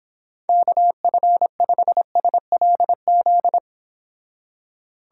Morse code, heard as a keyed tone, about 26 wpm, 700 Hz. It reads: KF5SLZ